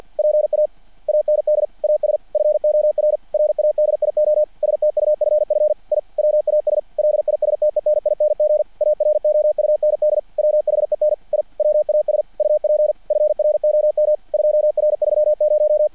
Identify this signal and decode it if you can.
Morse code, heard as a keyed tone, about 50 wpm, 600 Hz. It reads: PA AND AS FOR RADIO STUFF I WAS LISTENING AROUND WHEN I WAS UP FROM 2R30
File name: archived / CRSnet